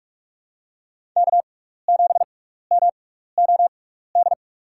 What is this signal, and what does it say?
Morse code, heard as a keyed tone, about 46 wpm, spaced as for 16 wpm, 700 Hz. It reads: K7MOD
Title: USA Call Signs - 6x Character Spacing - 45wpm